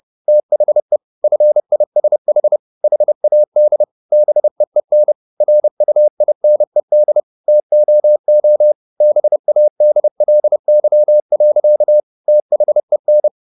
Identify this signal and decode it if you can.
Morse code, heard as a keyed tone, about 30 wpm, 600 Hz. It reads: THE FISH HAD BEEN RUINED TOO BADLY. THEN